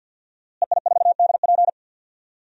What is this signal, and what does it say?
Morse code, heard as a keyed tone, about 45 wpm, 700 Hz. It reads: EI4DP
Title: DXCC-100 Call Signs - 5 Characters - Repeated 2x - 50wpm